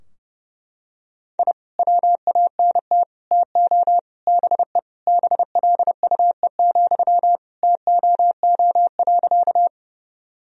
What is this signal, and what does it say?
Morse code, heard as a keyed tone, about 32 wpm, 700 Hz. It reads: I WANT TO BE BLUE, TOO.
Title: Sentences from Top 1000 Words - Review Part 3 - 30wpm